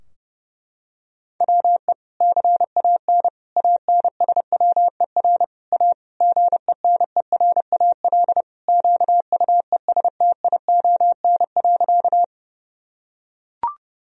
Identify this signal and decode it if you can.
Morse code, heard as a keyed tone, about 30 wpm, 700 Hz. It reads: WE CAN ANSWER A GENERAL QUESTION.
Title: Sentences from Top 1000 Words - Review Part 3 - 30wpm